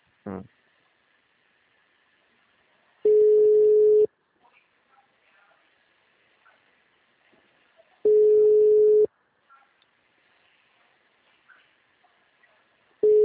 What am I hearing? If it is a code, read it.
Ringback tone (Europe, 425 Hz), with speech.
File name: out- -2002- - - .75269